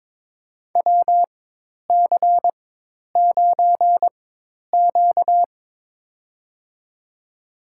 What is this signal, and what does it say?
Morse code, heard as a keyed tone, about 22 wpm, spaced as for 10 wpm, 700 Hz. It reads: WC9Q
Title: USA Call Signs - 4 Characters - 4x Character Spacing - 22wpm